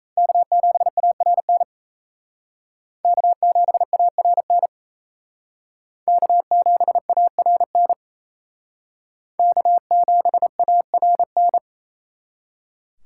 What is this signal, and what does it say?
Morse code, heard as a keyed tone, about 35 wpm, 700 Hz. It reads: K7ARN K7ARN K7ARN K7ARN